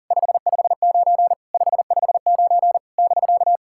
Morse code, heard as a keyed tone, about 37 wpm, 690 Hz. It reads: 559 559 <BK>
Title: SOTA QSO Practice 40wpm